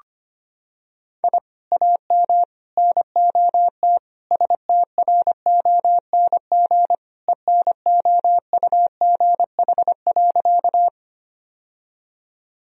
Morse code, heard as a keyed tone, about 25 wpm, 700 Hz. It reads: I AM NOT STRONG ENOUGH.
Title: Sentences from Top 1000 Words - Review Part 3 - 25wpm